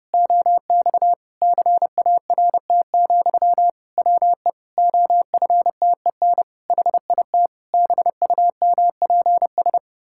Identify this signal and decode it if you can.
Morse code, heard as a keyed tone, about 30 wpm, 700 Hz. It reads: OX CART, WE OFTEN HIT BUMPS
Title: Pangram Challenge v2 - 30wpm